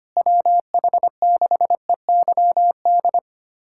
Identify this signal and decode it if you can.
Morse code, heard as a keyed tone, about 25 wpm, 700 Hz. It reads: WH6EYD
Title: USA Callsign DE Callsign 25wpm